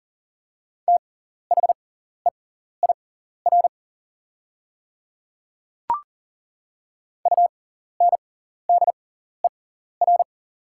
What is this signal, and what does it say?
Morse code, heard as a keyed tone, about 40 wpm, spaced as for 14 wpm, 690 Hz. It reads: THEIR UNDER
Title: 6x Character Spacing - Top 300 Words - 40wpm